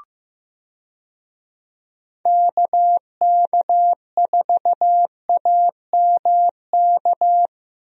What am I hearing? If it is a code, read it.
Morse code, heard as a keyed tone, about 15 wpm, 700 Hz. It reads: KK4AMK
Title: USA Call Signs - 6 Characters - Repeated 3x - 15wpm